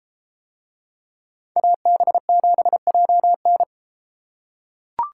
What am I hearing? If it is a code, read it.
Morse code, heard as a keyed tone, about 31 wpm, 700 Hz. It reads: AB7JN